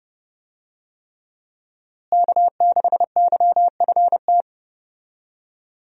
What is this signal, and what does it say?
Morse code, heard as a keyed tone, about 31 wpm, 700 Hz. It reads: K6YFT
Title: USA Call Signs - 5 Characters - Repeated 3x - 30wpm